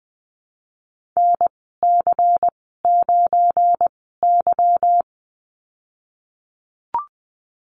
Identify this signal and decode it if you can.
Morse code, heard as a keyed tone, about 20 wpm, spaced as for 15 wpm, 700 Hz. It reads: NC9Y
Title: 200 Lesson 32g - Letter k - Call Signs 20wpm